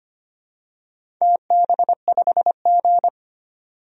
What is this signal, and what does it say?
Morse code, heard as a keyed tone, about 25 wpm, 700 Hz. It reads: TB5G
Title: DXCC-100 Call Signs - 4 Characters - 25wpm